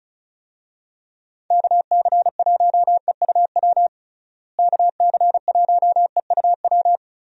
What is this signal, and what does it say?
Morse code, heard as a keyed tone, about 34 wpm, 700 Hz. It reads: KC1EUW KC1EUW
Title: USA Call Signs - 6 Characters - Repeated 3x - 35wpm